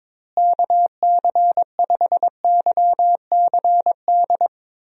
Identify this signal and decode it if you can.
Morse code, heard as a keyed tone, about 22 wpm, 700 Hz. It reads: KC5YCD
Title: USA Call Signs - 6 Characters - Repeated 2x - 22wpm